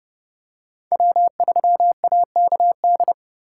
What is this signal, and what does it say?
Morse code, heard as a keyed tone, about 26 wpm, 700 Hz. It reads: W3AKD